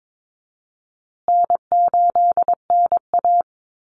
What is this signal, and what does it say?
Morse code, heard as a keyed tone, about 22 wpm, 700 Hz. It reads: N8NA
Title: Easy Call Signs Repeated 3x - 22wpm